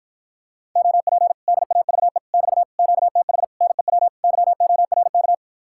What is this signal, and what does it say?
Morse code, heard as a keyed tone, about 54 wpm, 700 Hz. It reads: OP DAVE <BT> QTH NEW YORK